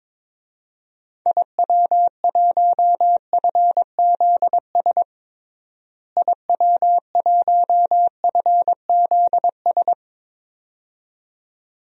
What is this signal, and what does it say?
Morse code, heard as a keyed tone, about 22 wpm, 700 Hz. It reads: IW1FZS IW1FZS
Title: DXCC-100 Call Signs - 6 Characters - Repeated 2x - 22wpm